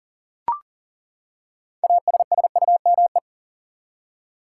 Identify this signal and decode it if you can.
Morse code, heard as a keyed tone, about 40 wpm, 700 Hz. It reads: ASSUME
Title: Six Letter Words - 40wpm